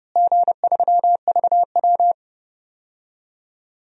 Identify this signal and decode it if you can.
Morse code, heard as a keyed tone, about 31 wpm, 700 Hz. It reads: G3VW
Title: DXCC-100 Call Signs - 4 Characters - Repeated 3x - 30wpm